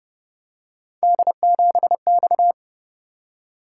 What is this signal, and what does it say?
Morse code, heard as a keyed tone, about 31 wpm, 700 Hz. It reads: D7X
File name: Speed-Racing - Three Letter-Number Combinations - 25wpm